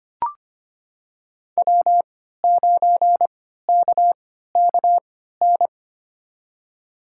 Morse code, heard as a keyed tone, about 25 wpm, spaced as for 14 wpm, 700 Hz. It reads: W9KKN
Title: USA Call Signs - 5 Characters - 3x Character Spacing - 25wpm